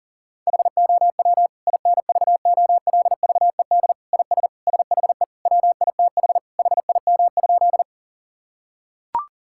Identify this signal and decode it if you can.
Morse code, heard as a keyed tone, about 40 wpm, 700 Hz. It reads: HOW INVOLVED IS SHE WITH HIM?
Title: Sentences from Top 1000 Words - Review Part 5 - 40wpm